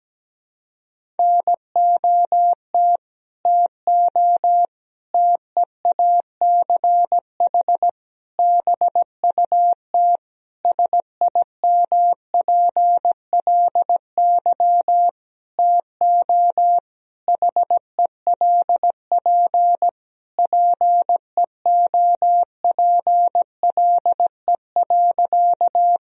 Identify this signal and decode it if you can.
Morse code, heard as a keyed tone, about 17 wpm, 700 Hz. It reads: NOT TO TEACH BUT SIMPLY TO HELP PEOPLE.
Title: Sentences from Top 1000 Words - Review Part 4 - 17wpm